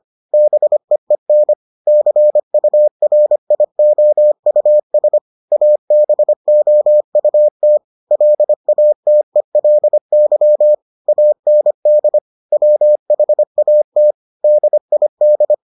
Morse code, heard as a keyed tone, about 25 wpm, 600 Hz. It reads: BEEN CURIOUS ABOUT LATELY AND WHAT DID